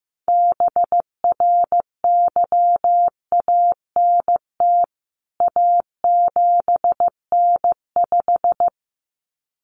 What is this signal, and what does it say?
Morse code, heard as a keyed tone, about 15 wpm, 700 Hz. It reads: BRYANT A7N5
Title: CWT Exchange 15wpm